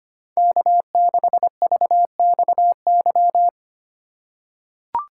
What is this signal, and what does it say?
Morse code, heard as a keyed tone, about 25 wpm, 700 Hz. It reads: K6VXY
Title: USA Call Signs - 5 Characters - 2x Character Spacing - 25wpm